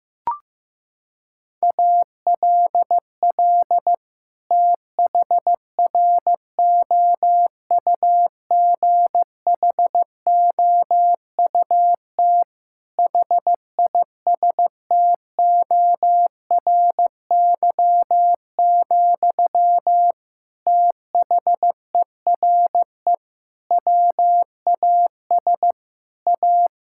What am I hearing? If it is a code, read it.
Morse code, heard as a keyed tone, about 16 wpm, 700 Hz. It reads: ALL THROUGHOUT HISTORY, THERE WAS A
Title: Sentences from Top 1000 Words - Review Part 5 - 15wpm